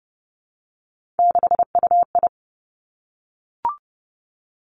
Morse code, heard as a keyed tone, about 30 wpm, 700 Hz. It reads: 6UI
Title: Three Challenging Letter-Number Combinations 30wpm